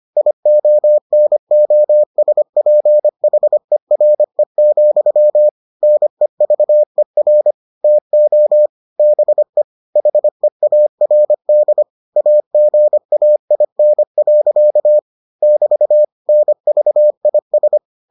Morse code, heard as a keyed tone, about 25 wpm, 600 Hz. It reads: IONOSPHERE, NEVER TO BE HEARD AGAIN. <BT> NVIS